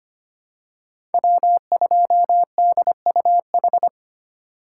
Morse code, heard as a keyed tone, about 25 wpm, 700 Hz. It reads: W2DUH
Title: USA Call Signs - 2x Character Spacing - 25wpm